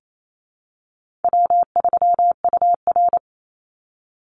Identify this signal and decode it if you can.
Morse code, heard as a keyed tone, about 28 wpm, 700 Hz. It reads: W3UR
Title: Easy Call Signs Repeated 3x - 28wpm